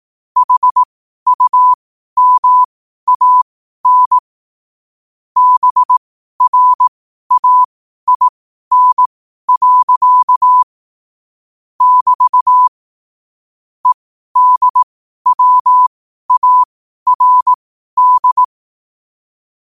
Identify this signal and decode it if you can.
Morse code, heard as a keyed tone, about 18 wpm, spaced as for 12 wpm, 1000 Hz. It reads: HUMAN BRAIN. <BT> EDWARD
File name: qotdfiles / 12wpm